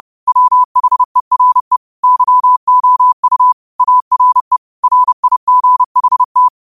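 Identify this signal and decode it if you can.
Morse code, heard as a keyed tone, about 30 wpm, 1000 Hz. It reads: WHERE YOU ARE RIGHT
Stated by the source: Morse code quotes of the day (QOTD) sent at 30 words per minute.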